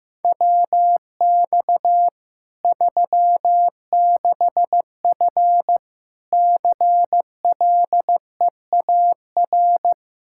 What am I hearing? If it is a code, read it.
Morse code, heard as a keyed tone, about 16 wpm, 700 Hz. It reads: WX 36F CLEAR
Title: Practice QSO - Exchange 2 - Info - 15wpm